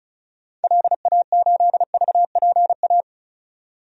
Morse code, heard as a keyed tone, about 34 wpm, 700 Hz. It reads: LA8VPA
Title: DXCC-100 Call Signs - 6 Characters - 35wpm